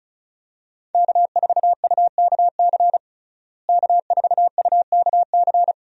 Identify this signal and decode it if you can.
Morse code, heard as a keyed tone, about 36 wpm, 700 Hz. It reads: K4UKC K4UKC
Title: USA Call Signs - 5 Characters - Repeated 3x - 35wpm